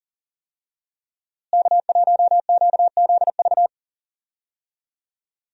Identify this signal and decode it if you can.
Morse code, heard as a keyed tone, about 40 wpm, 700 Hz. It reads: K1QZV
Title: USA Call Signs - 5 Characters - Repeated 2x - 40wpm